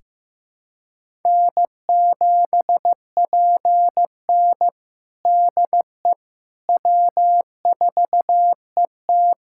Morse code, heard as a keyed tone, about 16 wpm, 700 Hz. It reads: N7PN DE W4ET
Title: USA Callsign DE Callsign - 4 Characters - 15wpm